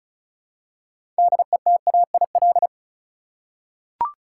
Morse code, heard as a keyed tone, about 34 wpm, 700 Hz. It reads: DETAIL
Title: Six Letter Words - 35wpm